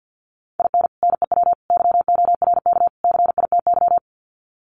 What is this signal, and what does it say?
Morse code, heard as a keyed tone, about 50 wpm, 700 Hz. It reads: IN NEW YORK CITY
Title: Sets of 4 Words 50wpm